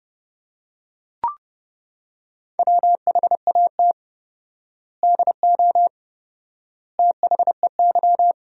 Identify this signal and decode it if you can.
Morse code, heard as a keyed tone, about 30 wpm, 700 Hz. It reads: WHAT DO THEY
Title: 4x Word-Spacing - Sentences from Top 200 Words - 30wpm